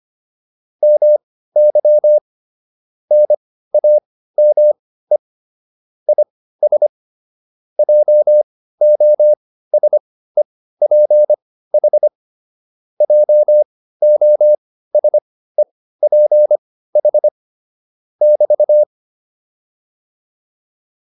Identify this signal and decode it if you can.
Morse code, heard as a keyed tone, about 25 wpm, spaced as for 15 wpm, 600 Hz. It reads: MY NAME IS JOSEPH JOSEPH <BT>